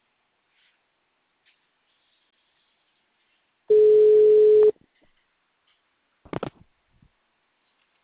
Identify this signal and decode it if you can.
Ringback tone (Europe, 425 Hz).
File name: out- -2056- - - .3502